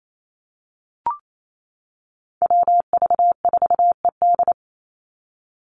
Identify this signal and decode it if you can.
Morse code, heard as a keyed tone, about 28 wpm, 700 Hz. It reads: WV4ED
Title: USA Call Signs 28wpm